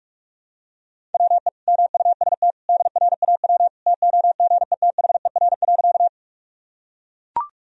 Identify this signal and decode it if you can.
Morse code, heard as a keyed tone, about 42 wpm, 700 Hz. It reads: WE MUST DRAW TOGETHER.
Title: Sentences from Top 1000 Words - Review Part 1 - 45wpm